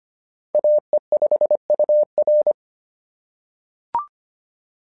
Morse code, heard as a keyed tone, about 25 wpm, 620 Hz. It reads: AE5UR
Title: SOTA QSO Practice 25wpm